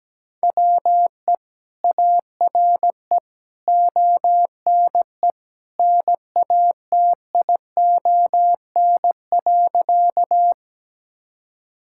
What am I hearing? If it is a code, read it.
Morse code, heard as a keyed tone, about 17 wpm, 700 Hz. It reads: WE ARE ONE NATION.
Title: Sentences from Top 1000 Words - Review Part 6 - 17wpm